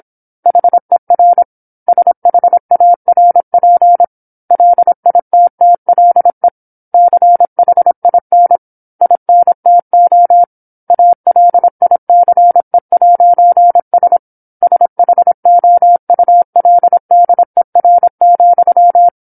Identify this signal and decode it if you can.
Morse code, heard as a keyed tone, about 27 wpm, 700 Hz. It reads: HER SHARP LITTLE CHIN INTO ALICE'S SHOULDER,
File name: 08-Chapter-08 26 WPM